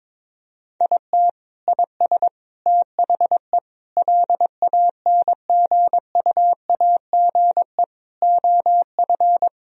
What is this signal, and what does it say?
Morse code, heard as a keyed tone, about 22 wpm, 700 Hz. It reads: IT IS THE LANGUAGE OF
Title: Sentences from Top 1000 Words - Review Part 3 - 22wpm